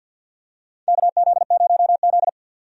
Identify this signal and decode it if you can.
Morse code, heard as a keyed tone, about 52 wpm, 700 Hz. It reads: KZ0Z
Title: USA Call Signs - 4 Characters - 50wpm